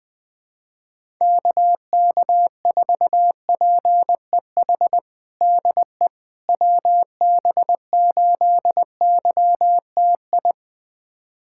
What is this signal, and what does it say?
Morse code, heard as a keyed tone, about 21 wpm, 700 Hz. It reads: KK4PEH DE WB8YTI
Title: USA Callsign DE Callsign - 6 Characters - 20wpm